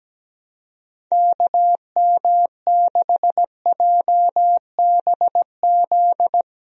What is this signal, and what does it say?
Morse code, heard as a keyed tone, about 17 wpm, 700 Hz. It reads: KM6JBZ
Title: USA Call Signs - 6 Characters - Repeated 2x - 17wpm